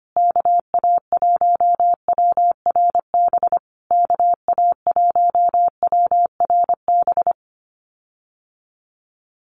Morse code, heard as a keyed tone, about 25 wpm, 700 Hz. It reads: KA1WRB KA1WRB
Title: USA Call Signs Repeated 3x 25wpm